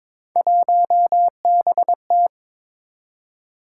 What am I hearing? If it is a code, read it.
Morse code, heard as a keyed tone, about 22 wpm, 700 Hz. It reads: 1BT
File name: practice / 22wpm / Three Letter-Number Combinations - 22wpm